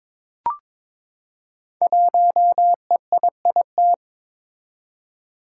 Morse code, heard as a keyed tone, about 22 wpm, 700 Hz. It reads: 1EIIT
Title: Five Challenging Letter-Number Combinations 22wpm